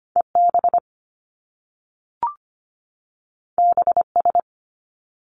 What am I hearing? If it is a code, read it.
Morse code, heard as a keyed tone, about 25 wpm, 700 Hz. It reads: EB BS
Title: Top Two Letter Combinations in Words 25wpm